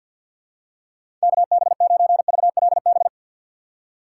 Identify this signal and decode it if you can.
Morse code, heard as a keyed tone, about 49 wpm, 700 Hz. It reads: KB9VLB
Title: USA Call Signs - 6x Character Spacing - 50wpm